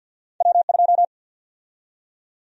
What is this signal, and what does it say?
Morse code, heard as a keyed tone, about 53 wpm, 700 Hz. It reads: W2
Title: USA Call Sign Prefixes - 50wpm